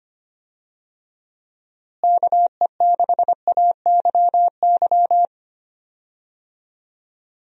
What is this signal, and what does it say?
Morse code, heard as a keyed tone, about 25 wpm, 700 Hz. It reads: KE6AYY
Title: USA Call Signs - 6 Characters - Repeated 2x - 25wpm